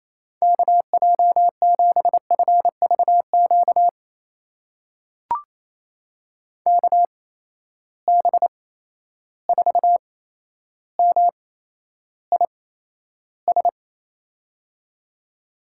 Morse code, heard as a keyed tone, about 28 wpm, spaced as for 12 wpm, 700 Hz. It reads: KJ7FVQ KB4MIS
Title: USA Call Signs - 6 Characters - 8x Character Spacing - 28wpm